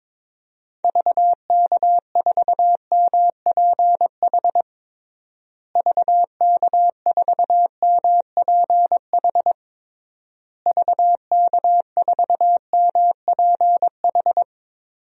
Morse code, heard as a keyed tone, about 23 wpm, 700 Hz. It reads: VK4MPH VK4MPH VK4MPH
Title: DXCC-100 Call Signs - 6 Characters - Repeated 3x - 22wpm